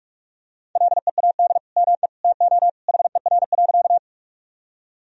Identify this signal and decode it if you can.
Morse code, heard as a keyed tone, about 42 wpm, 700 Hz. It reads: LEAD ME TO HER.
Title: Sentences from Top 1000 Words - Review Part 2 - 45wpm